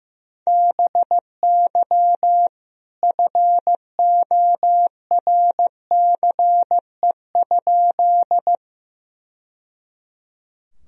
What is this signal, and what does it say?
Morse code, heard as a keyed tone, about 15 wpm, 700 Hz. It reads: BY FORCE?
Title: Sentences from Top 1000 Words - Review Part 3 - 15wpm